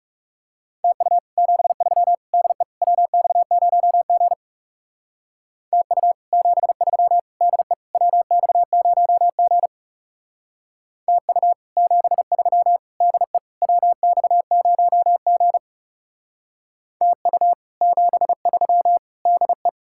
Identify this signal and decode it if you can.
Morse code, heard as a keyed tone, about 38 wpm, 700 Hz. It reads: TU 73 DE WX0G TU 73 DE WX0G TU 73 DE WX0G TU 73 DE